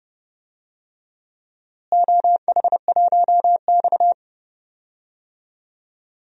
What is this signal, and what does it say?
Morse code, heard as a keyed tone, about 30 wpm, 700 Hz. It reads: OH1X
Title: DXCC-100 Call Signs - 4 Characters - Repeated 3x - 30wpm